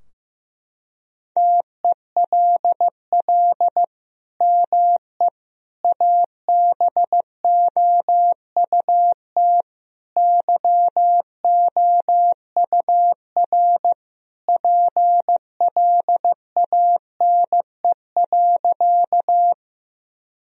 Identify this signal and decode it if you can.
Morse code, heard as a keyed tone, about 16 wpm, 700 Hz. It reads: TELL ME ABOUT YOUR PLANE.
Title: Sentences from Top 1000 Words - Review Part 3 - 15wpm